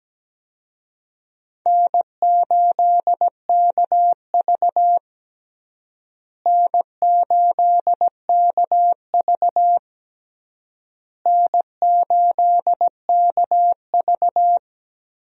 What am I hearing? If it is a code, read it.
Morse code, heard as a keyed tone, about 17 wpm, 700 Hz. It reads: N8KV N8KV N8KV
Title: USA Call Signs - 4 Characters - Repeated 3x - 17wpm